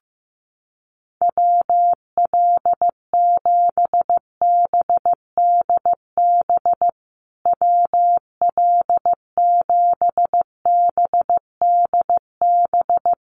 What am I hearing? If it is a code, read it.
Morse code, heard as a keyed tone, about 15 wpm, 700 Hz. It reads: WL7BDB WL7BDB
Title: USA Call Signs Repeated 2x 15wpm